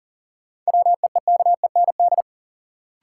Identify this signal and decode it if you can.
Morse code, heard as a keyed tone, about 40 wpm, 700 Hz. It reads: WEEKEND
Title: Sets of 5 Words - New Word Drill - 40wpm